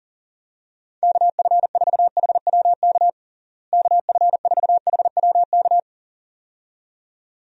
Morse code, heard as a keyed tone, about 37 wpm, 700 Hz. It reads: KF4HWK KF4HWK
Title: USA Call Signs - 6 Characters - Repeated 2x - 40wpm